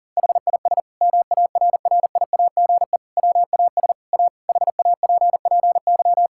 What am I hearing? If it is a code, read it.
Morse code, heard as a keyed tone, about 40 wpm, 700 Hz. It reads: HIS MARRIAGE WAS A HAPPY
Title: Sentences from Top 1000 Words - Review Part 5 - 40wpm